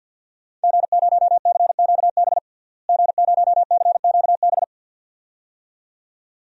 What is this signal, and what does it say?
Morse code, heard as a keyed tone, about 53 wpm, 700 Hz. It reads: G0CQB G0CQB
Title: DXCC-100 Call Signs - 5 Characters - Repeated 2x - 50wpm